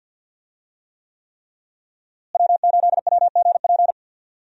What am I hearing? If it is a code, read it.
Morse code, heard as a keyed tone, about 52 wpm, 700 Hz. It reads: W8WGP
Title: USA Call Signs - 5 Characters - Repeated 3x - 50wpm